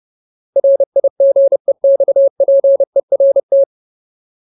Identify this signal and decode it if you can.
Morse code, heard as a keyed tone, about 30 wpm, 550 Hz. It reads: RIGEXPERT
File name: Ultimate Word List / 30WPM Ultimate